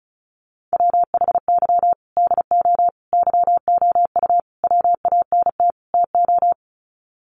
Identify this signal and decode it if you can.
Morse code, heard as a keyed tone, about 34 wpm, 700 Hz. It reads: WHY DO YOU WANT TO
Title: Sets of 5 Words Encore2 35wpm